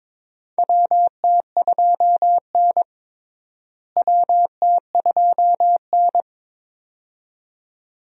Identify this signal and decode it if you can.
Morse code, heard as a keyed tone, about 22 wpm, 700 Hz. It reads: WT2N WT2N
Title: USA Call Signs - 4 Characters - Repeated 2x - 22wpm